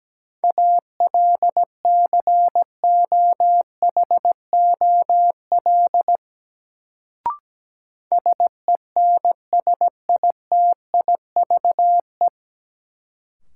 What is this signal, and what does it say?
Morse code, heard as a keyed tone, about 17 wpm, 700 Hz. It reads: ALCOHOL SENSITIVE
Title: Sentences from Top 1400 Words - New Word Drill - 17wpm